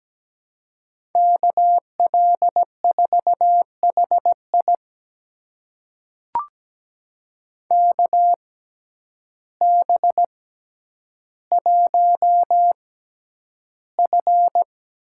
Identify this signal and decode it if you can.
Morse code, heard as a keyed tone, about 17 wpm, spaced as for 9 wpm, 700 Hz. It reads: KL4HI KB1F
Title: USA Call Signs - 5 Characters - 6x Character Spacing - 17wpm